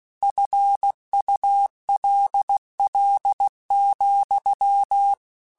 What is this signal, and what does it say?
Morse code, heard as a keyed tone, about 16 wpm, 800 Hz. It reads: FULL,
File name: Chuckles / Chuckles-mp3s / Glass Half Full @15